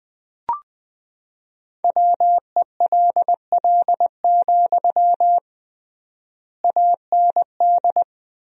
Morse code, heard as a keyed tone, about 21 wpm, 700 Hz. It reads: WELL, AND
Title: 3x Word-Spacing - Sentences from Top 200 Words - 20wpm